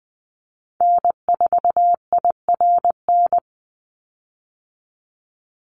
Morse code, heard as a keyed tone, about 20 wpm, 700 Hz. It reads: N4IRN
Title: Call Signs Repeated 3x - 20wpm